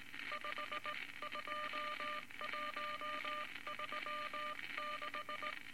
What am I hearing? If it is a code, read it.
Morse code, heard as a keyed tone, about 19 wpm, 1200 Hz, with noise. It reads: 52136
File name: Morse Code Records / 19 B LCM 167 Twelve Words Per Minute